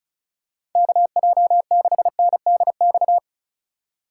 Morse code, heard as a keyed tone, about 36 wpm, 700 Hz. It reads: KJ6NDX
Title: USA Call Signs - 6 Characters - 2x Character Spacing - 35wpm